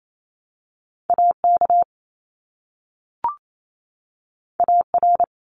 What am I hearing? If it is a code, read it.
Morse code, heard as a keyed tone, about 28 wpm, 700 Hz. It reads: AK AR
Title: US States 28wpm